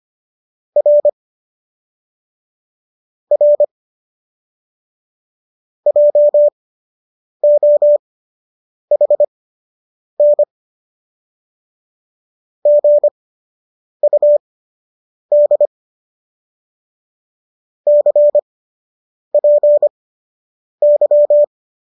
Morse code, heard as a keyed tone, about 25 wpm, spaced as for 8 wpm, 590 Hz. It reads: R R JOHN GUD CPY
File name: qso006 08